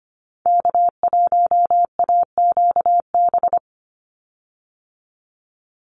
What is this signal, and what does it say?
Morse code, heard as a keyed tone, about 25 wpm, 700 Hz. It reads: K1AQB
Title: Call Signs Repeated 3x - 25wpm